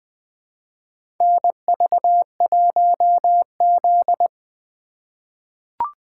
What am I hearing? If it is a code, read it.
Morse code, heard as a keyed tone, about 20 wpm, 700 Hz. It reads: NV1Z
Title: USA Call Signs - 4 Characters - 2x Character Spacing - 20wpm